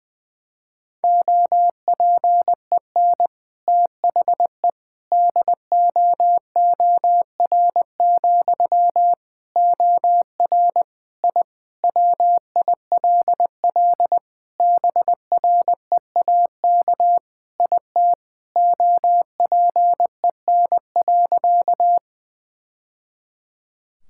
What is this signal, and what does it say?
Morse code, heard as a keyed tone, about 20 wpm, 700 Hz. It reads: OPEN THE DOOR, OR I WILL BREAK IT OPEN.
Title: Sentences from Top 1000 Words - Review Part 6 - 20wpm